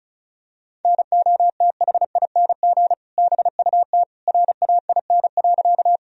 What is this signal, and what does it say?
Morse code, heard as a keyed tone, about 36 wpm, 700 Hz. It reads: NOTHING BUT RAIN.
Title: Sentences from Top 1000 Words - Review Part 1 - 35wpm